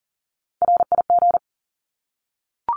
Morse code, heard as a keyed tone, about 40 wpm, 700 Hz. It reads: RIG
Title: Abbreviations 40wpm